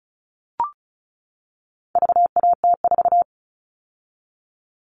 Morse code, heard as a keyed tone, about 34 wpm, 700 Hz. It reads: VAT4
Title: All Cut Numbers - 4-Digit 35wpm